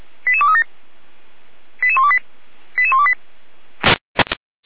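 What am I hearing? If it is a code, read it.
Five-tone selective call (ZVEI), heard as quick runs of five tones, tones of about 70 ms. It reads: 88237 88128 88128
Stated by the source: Five-tone ZVEI selective call received by SDR, from the audio archive of feuerwehr-orlamuende.org (fire brigade).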